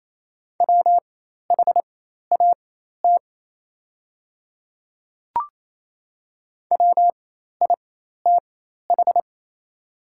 Morse code, heard as a keyed tone, about 28 wpm, spaced as for 13 wpm, 700 Hz. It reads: WHAT WITH
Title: 4x Character Spacing - Top 100 Words - 28wpm